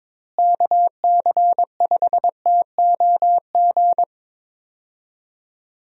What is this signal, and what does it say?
Morse code, heard as a keyed tone, about 23 wpm, 700 Hz. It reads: KC5TOG
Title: USA Call Signs - 6 Characters - Repeated 2x - 22wpm